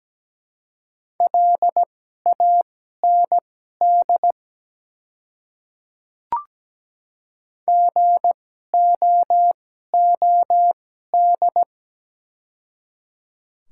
Morse code, heard as a keyed tone, about 17 wpm, spaced as for 12 wpm, 700 Hz. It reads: LAND GOOD
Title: 2x Character Spacing - Top 200 Words - 17wpm